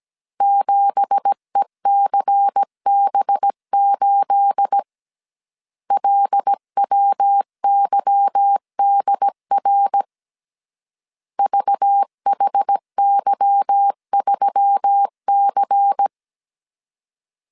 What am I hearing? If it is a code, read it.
Morse code, heard as a keyed tone, about 17 wpm, 800 Hz. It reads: 7ECB8 LWYDR VHY3C
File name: CW / DET / 1 9 1500C S C 16wpm 2909